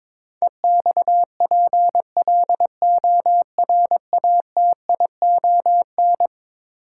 Morse code, heard as a keyed tone, about 22 wpm, 700 Hz. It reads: EXPLORATION
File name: practice / 22wpm / Eleven Letter Words - 22wpm